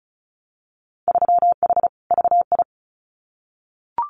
Morse code, heard as a keyed tone, about 34 wpm, 700 Hz. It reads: 3H VI
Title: Winter Field Day Exchanges Encore 35wpm